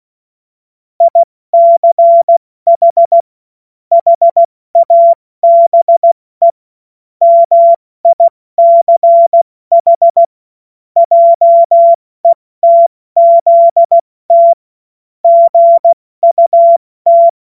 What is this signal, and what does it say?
Morse code, heard as a keyed tone, about 16 wpm, spaced as for 14 wpm, 680 Hz. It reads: ICH HABE MICH JETZT GUT
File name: Morsekurs / Witze / Witz 02. Mai 2023 70 F80